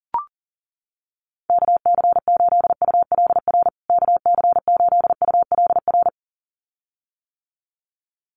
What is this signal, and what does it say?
Morse code, heard as a keyed tone, about 40 wpm, 700 Hz. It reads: KC8ULR KC8ULR
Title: USA Call Signs Repeated 2x 40wpm